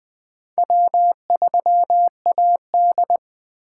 Morse code, heard as a keyed tone, about 20 wpm, 700 Hz. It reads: W3AD
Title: USA Call Signs - 4 Characters - 3x Character Spacing - 20wpm